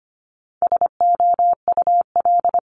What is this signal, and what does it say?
Morse code, heard as a keyed tone, about 25 wpm, 700 Hz. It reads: SOUL
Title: Multi-letter Phonemes 25wpm